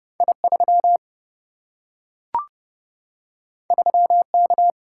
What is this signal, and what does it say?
Morse code, heard as a keyed tone, about 30 wpm, 700 Hz. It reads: I 3 3 K
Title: Single-Letter Single-Number 30wpm